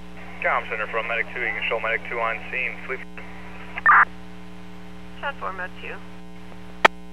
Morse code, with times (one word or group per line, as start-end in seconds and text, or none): none